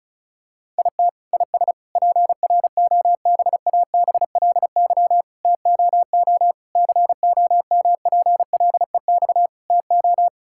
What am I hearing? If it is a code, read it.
Morse code, heard as a keyed tone, about 36 wpm, 700 Hz. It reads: IT IS PROBABLY TOO COMPLEX TO
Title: Sentences from Top 1000 Words - Review Part 6 - 35wpm